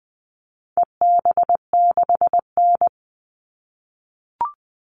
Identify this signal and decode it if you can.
Morse code, heard as a keyed tone, about 20 wpm, 700 Hz. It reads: EB6N
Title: All Cut Numbers - 4-Digit 20wpm